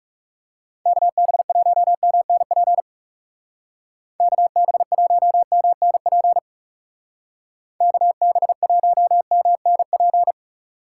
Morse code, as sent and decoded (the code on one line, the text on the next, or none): -.- -... .---- -- -. .--. / -.- -... .---- -- -. .--. / -.- -... .---- -- -. .--.
KB1MNP KB1MNP KB1MNP